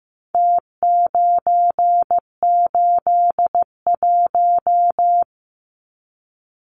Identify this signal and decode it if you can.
Morse code, heard as a keyed tone, about 15 wpm, 700 Hz. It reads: T981
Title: Numbers - Contest Style - 4-Digit 15wpm